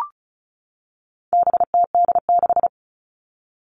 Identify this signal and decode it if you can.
Morse code, heard as a keyed tone, about 34 wpm, 700 Hz. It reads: BTD6
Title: All Cut Numbers - 4-Digit 35wpm